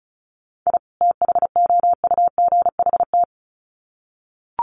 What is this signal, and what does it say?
Morse code, heard as a keyed tone, about 34 wpm, 700 Hz. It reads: I THOUGHT
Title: Sets of 2 Words 35wpm